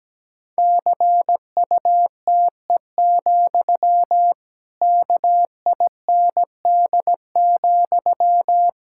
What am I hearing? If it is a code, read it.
Morse code, heard as a keyed tone, about 17 wpm, 700 Hz. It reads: CUTE, KIND,
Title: Pangram Challenge v2 - 17wpm